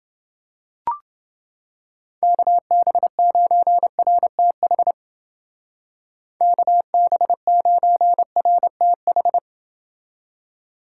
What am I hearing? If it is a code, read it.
Morse code, heard as a keyed tone, about 28 wpm, 700 Hz. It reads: KB9RTH KB9RTH